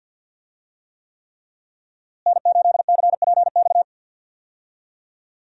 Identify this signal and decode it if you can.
Morse code, heard as a keyed tone, about 51 wpm, 700 Hz. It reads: N8CPX
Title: USA Call Signs - 5 Characters - Repeated 3x - 50wpm